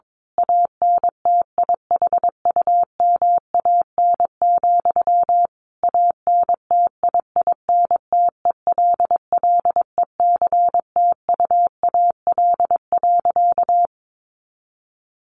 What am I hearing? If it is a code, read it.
Morse code, heard as a keyed tone, about 22 wpm, 700 Hz. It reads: ANTIHUMAN, ANTIINTELLECTUAL.
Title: A Sound Of Thunder 22wpm